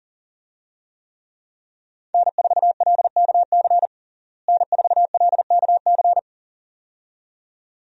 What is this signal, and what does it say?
Morse code, heard as a keyed tone, about 37 wpm, 700 Hz. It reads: N4LKC N4LKC
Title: USA Call Signs - 5 Characters - Repeated 2x - 40wpm